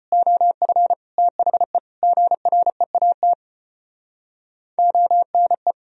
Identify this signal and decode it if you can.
Morse code, heard as a keyed tone, about 31 wpm, 700 Hz. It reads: OF THE GREAT ONE
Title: Speed-Racing - Sets of 4 Words 25wpm